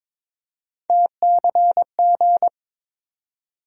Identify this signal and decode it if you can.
Morse code, heard as a keyed tone, about 22 wpm, 700 Hz. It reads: TCG
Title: USA Call Sign Postfixes - 3 Letters 22wpm